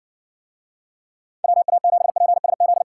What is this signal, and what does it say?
Morse code, heard as a keyed tone, about 67 wpm, 700 Hz. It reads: WA7PSZ